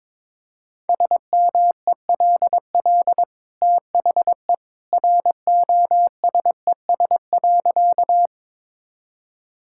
Morse code, heard as a keyed tone, about 22 wpm, 690 Hz. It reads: SMELL THE ROSES.
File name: practice / 22wpm / Easy Sentences with Hints - 22wpm